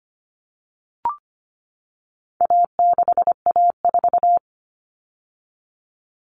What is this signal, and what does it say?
Morse code, heard as a keyed tone, about 25 wpm, 700 Hz. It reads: A6A4
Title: All Cut Numbers - 4-Digit 25wpm